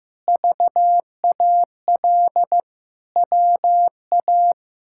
Morse code, heard as a keyed tone, about 16 wpm, 700 Hz. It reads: VAL WA
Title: K1USN SST QSO Practice - 15wpm